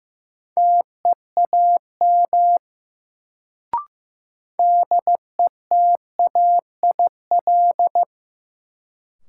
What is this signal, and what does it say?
Morse code, heard as a keyed tone, about 15 wpm, 700 Hz. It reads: TEAM DETAIL
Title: Sentences from Top 1500 Words - New Word Drill - 15wpm